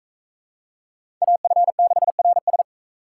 Morse code, heard as a keyed tone, about 41 wpm, 700 Hz. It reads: AF6RS